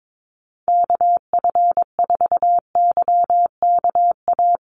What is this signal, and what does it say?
Morse code, heard as a keyed tone, about 22 wpm, 700 Hz. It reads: KF4YKA
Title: USA Call Signs Repeated 2x 22wpm